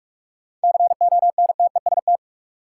Morse code, heard as a keyed tone, about 47 wpm, 700 Hz. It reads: CONTEST
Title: Sentences from Top 1600 Words - New Word Drill - 45wpm